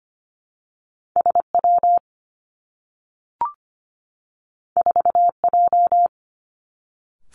Morse code, heard as a keyed tone, about 25 wpm, 700 Hz. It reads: S W 4 J
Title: Two Challenging Letter-Number Combinations 25wpm